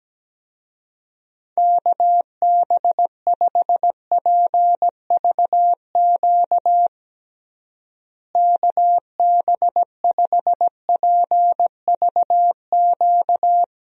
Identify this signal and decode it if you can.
Morse code, heard as a keyed tone, about 17 wpm, 700 Hz. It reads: KB5PVQ KB5PVQ
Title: USA Call Signs - 6 Characters - Repeated 3x - 17wpm